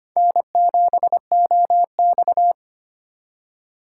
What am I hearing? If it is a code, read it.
Morse code, heard as a keyed tone, about 25 wpm, 700 Hz. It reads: N7OX
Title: USA Call Signs - 4 Characters - 2x Character Spacing - 25wpm